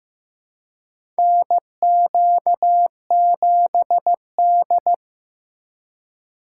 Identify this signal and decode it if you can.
Morse code, heard as a keyed tone, about 15 wpm, 700 Hz. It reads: NQ7D
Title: USA Call Signs - 4 Characters - 2x Character Spacing - 15wpm